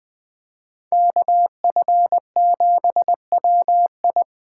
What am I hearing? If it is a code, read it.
Morse code, heard as a keyed tone, about 21 wpm, 700 Hz. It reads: KF7WI
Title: 2023 SOTA Chaser Call Signs - 20wpm